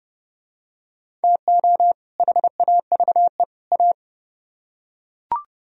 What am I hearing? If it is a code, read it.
Morse code, heard as a keyed tone, about 29 wpm, 700 Hz. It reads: TO HAVE A